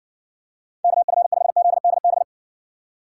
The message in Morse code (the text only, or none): KF5ZDB